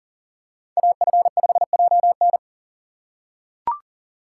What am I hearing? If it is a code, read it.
Morse code, heard as a keyed tone, about 40 wpm, 700 Hz. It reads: AF5JN
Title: USA Call Signs - 4x Character Spacing - 40wpm